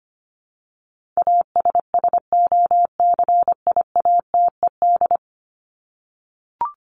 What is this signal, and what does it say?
Morse code, heard as a keyed tone, about 25 wpm, 700 Hz. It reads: ASSOCIATED
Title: ED Suffix - Top 100 Words with Base Words 25wpm